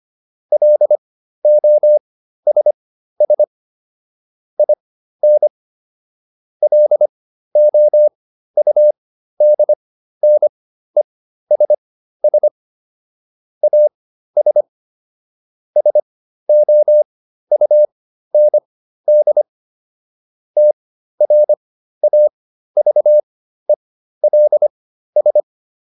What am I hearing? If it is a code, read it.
Morse code, heard as a keyed tone, about 25 wpm, spaced as for 13 wpm, 600 Hz. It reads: LOSS IN LOUDNESS AS SOUND TRAVELS